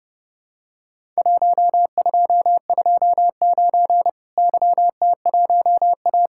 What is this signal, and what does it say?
Morse code, heard as a keyed tone, about 30 wpm, 700 Hz. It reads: 1229 YT1A
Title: Worked All Europe DX Contest - QTC Exchange - 30wpm